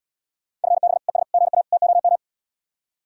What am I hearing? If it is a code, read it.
Morse code, heard as a keyed tone, about 79 wpm, 700 Hz. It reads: FB ES GA TOM